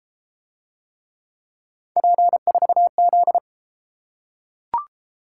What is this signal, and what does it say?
Morse code, heard as a keyed tone, about 30 wpm, 700 Hz. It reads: P4Z